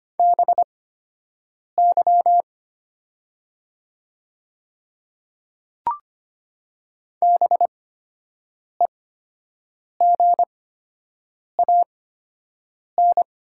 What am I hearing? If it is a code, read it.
Morse code, heard as a keyed tone, about 25 wpm, spaced as for 7 wpm, 700 Hz. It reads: BY BEGAN
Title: 8x Character Spacing - Top 300 Words - 25wpm